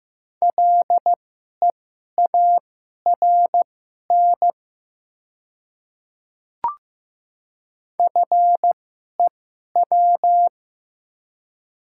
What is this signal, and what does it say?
Morse code, heard as a keyed tone, about 15 wpm, spaced as for 11 wpm, 700 Hz. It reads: LEARN FEW
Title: 2x Character Spacing - Top 300 Words - 15wpm